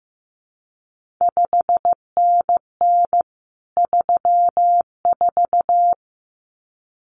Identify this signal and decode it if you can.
Morse code, heard as a keyed tone, about 15 wpm, 700 Hz. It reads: 5NN 34
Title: CQ WW DX Contest Exchange Encore - 15wpm